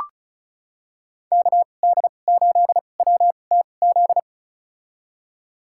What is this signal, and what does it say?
Morse code, heard as a keyed tone, about 34 wpm, spaced as for 25 wpm, 700 Hz. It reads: KD8WTZ
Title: USA Call Signs - 2x Character Spacing - 35wpm